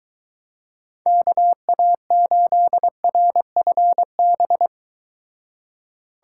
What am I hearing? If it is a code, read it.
Morse code, heard as a keyed tone, about 24 wpm, 700 Hz. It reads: KA8RFB